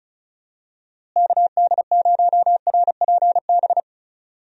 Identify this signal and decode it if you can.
Morse code, heard as a keyed tone, about 36 wpm, 700 Hz. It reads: KD0RPB
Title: USA Call Signs - 6 Characters - Repeated 3x - 35wpm